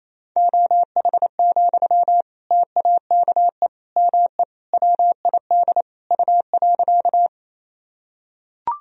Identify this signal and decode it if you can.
Morse code, heard as a keyed tone, about 28 wpm, 700 Hz. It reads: OH, TAKE ME WID U.
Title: Sentences from Top 500 Words and Abbreviations 28wpm